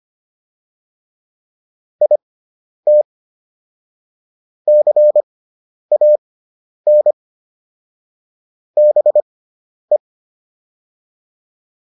Morse code, heard as a keyed tone, about 25 wpm, spaced as for 10 wpm, 600 Hz. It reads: IT CAN BE